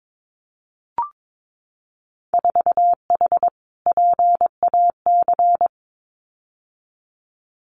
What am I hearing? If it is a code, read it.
Morse code, heard as a keyed tone, about 22 wpm, 700 Hz. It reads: 4H PAC
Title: Winter Field Day Exchanges Encore 22wpm